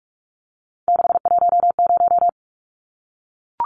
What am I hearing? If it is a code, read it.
Morse code, heard as a keyed tone, about 44 wpm, 700 Hz. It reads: 610
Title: Numbers - 3-Digit 45wpm